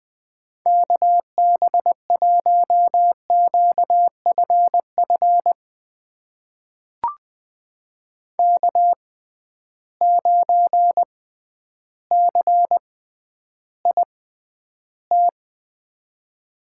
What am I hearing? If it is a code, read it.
Morse code, heard as a keyed tone, about 20 wpm, spaced as for 11 wpm, 700 Hz. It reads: KB1QFF K9CIT
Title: USA Call Signs - 6x Character Spacing - 20wpm